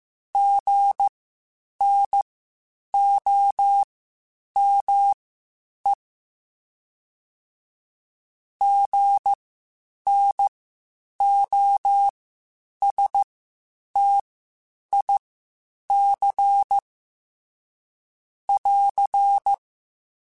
Morse code, heard as a keyed tone, about 15 wpm, spaced as for 8 wpm, 800 Hz. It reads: GNOME GNOSTIC <AR>
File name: Spaced Alphas / Spaced Alphas mp3s / Gs sp @13